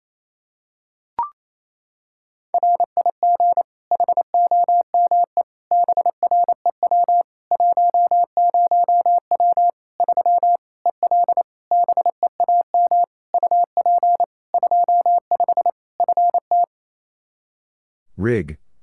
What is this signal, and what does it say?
Morse code, heard as a keyed tone, about 28 wpm, 700 Hz. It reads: RIG HOME BREW 10W 3 EL BEAM UP 25 FT
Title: K1USN SST QSO Practice 28wpm